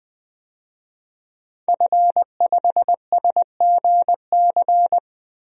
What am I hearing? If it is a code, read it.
Morse code, heard as a keyed tone, about 21 wpm, 700 Hz. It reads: F5SGC
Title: DXCC-100 Call Signs - 5 Characters - 20wpm